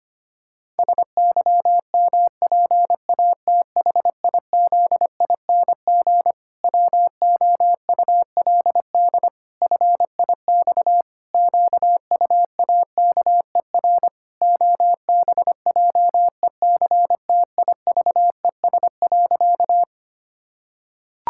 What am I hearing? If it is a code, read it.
Morse code, heard as a keyed tone, about 25 wpm, 700 Hz. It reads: SYMPATHIZING WOULD FIX QUAKER OBJECTIVES.
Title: Pangram Challenge v2 - 25wpm